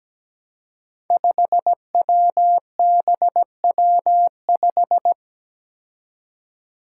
Morse code, heard as a keyed tone, about 17 wpm, 700 Hz. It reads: 5WBW5
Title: Five Challenging Letter-Number Combinations 17wpm